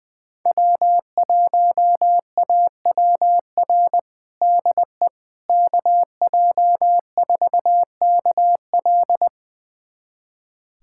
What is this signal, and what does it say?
Morse code, heard as a keyed tone, about 20 wpm, 700 Hz. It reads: W1AWR DE KJ4KL
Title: USA Callsign DE Callsign - 5 Characters - 20wpm